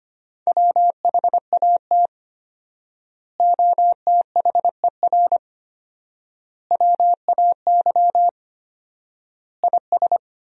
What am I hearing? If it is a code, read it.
Morse code, heard as a keyed tone, about 25 wpm, 700 Hz. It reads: WHAT OTHER WAY IS
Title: 4x Word-Spacing - Sentences from Top 100 Words - 25wpm